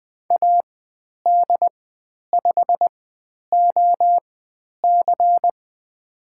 Morse code, heard as a keyed tone, about 20 wpm, spaced as for 10 wpm, 700 Hz. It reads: AD5OC
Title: 109 Lesson 18g - Letter m - Call Signs 20wpm